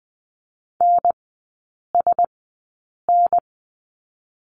Morse code, heard as a keyed tone, about 20 wpm, 700 Hz. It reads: N S N
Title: 035 Lesson 7e - Letter s - Three Characters 20wpm